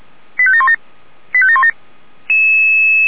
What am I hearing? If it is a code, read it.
Five-tone selective call (ZVEI), heard as quick runs of five tones, tones of about 70 ms. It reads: 85717 85717